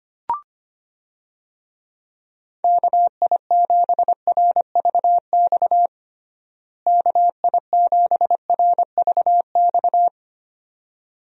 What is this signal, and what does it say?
Morse code, heard as a keyed tone, about 25 wpm, 700 Hz. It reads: KI7RVX KI7RVX
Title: USA Call Signs - 6 Characters - Repeated 2x - 25wpm